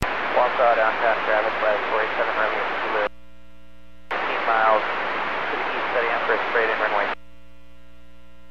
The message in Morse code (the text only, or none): none